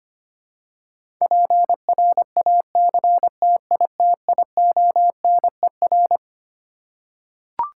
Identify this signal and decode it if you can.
Morse code, heard as a keyed tone, about 25 wpm, 700 Hz. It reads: PRACTITIONER
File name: Twelve Letter Words - 25wpm